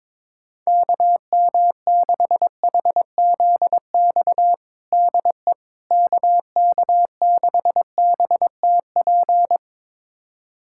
Morse code, heard as a keyed tone, about 21 wpm, 700 Hz. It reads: KM6HZX DE KK6BTP
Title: USA Callsign DE Callsign 22wpm